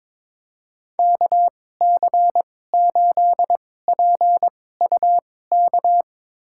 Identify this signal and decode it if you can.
Morse code, heard as a keyed tone, about 22 wpm, spaced as for 16 wpm, 700 Hz. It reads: KC8PUK
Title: USA Call Signs - 2x Character Spacing - 22wpm